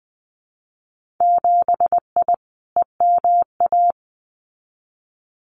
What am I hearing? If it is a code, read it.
Morse code, heard as a keyed tone, about 20 wpm, 700 Hz. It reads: 7I EMA
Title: Winter Field Day Exchanges Encore 20wpm